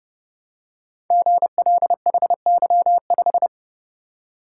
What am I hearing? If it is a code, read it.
Morse code, heard as a keyed tone, about 30 wpm, 700 Hz. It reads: GLHY5
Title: Code Groups - 5 Letter:Numbers - 30wpm